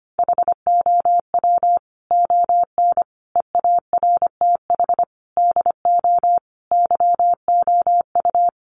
Morse code, heard as a keyed tone, about 25 wpm, 700 Hz. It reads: HOW ON EARTH DO YOU
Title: Sentences from Top 200 Words 25wpm